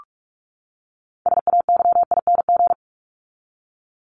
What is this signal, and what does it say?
Morse code, heard as a keyed tone, about 44 wpm, 700 Hz. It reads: SAYING
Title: ING Suffix - Top 100 Words with Base Words 45wpm